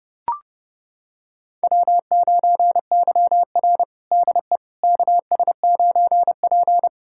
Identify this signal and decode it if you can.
Morse code, heard as a keyed tone, about 32 wpm, 700 Hz. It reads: W9YR DE KS9P
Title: USA Callsign DE Callsign - 4 Characters - 30wpm